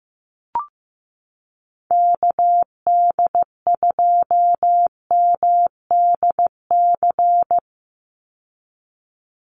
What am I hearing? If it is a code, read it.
Morse code, heard as a keyed tone, about 15 wpm, 700 Hz. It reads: KD2MDC
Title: USA Call Signs 15wpm